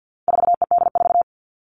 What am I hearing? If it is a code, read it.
Morse code, heard as a keyed tone, about 50 wpm, 700 Hz. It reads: 4ED4
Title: All Cut Numbers - 4-Digit 50wpm